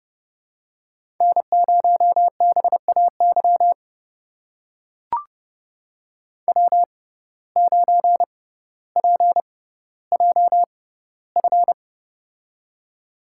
Morse code, heard as a keyed tone, about 32 wpm, spaced as for 16 wpm, 700 Hz. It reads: N0BAY W9PJF
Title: USA Call Signs - 5 Characters - 6x Character Spacing - 30wpm